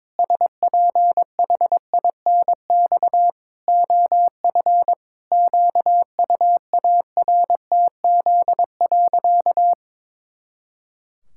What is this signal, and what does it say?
Morse code, heard as a keyed tone, about 22 wpm, 700 Hz. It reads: SPHINX OF QUARTZ.
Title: Pangram Challenge v2 - 22wpm